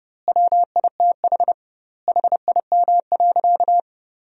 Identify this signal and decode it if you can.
Morse code, heard as a keyed tone, about 30 wpm, 700 Hz. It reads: WITH HIM.
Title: 2x Word-Spacing - Sentences from Top 300 Words - 30wpm